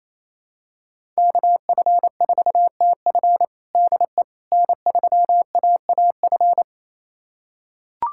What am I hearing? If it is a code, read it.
Morse code, heard as a keyed tone, about 27 wpm, 700 Hz. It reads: KF4TF DE N3AAF
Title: USA Callsign DE Callsign - 5 Characters - 28wpm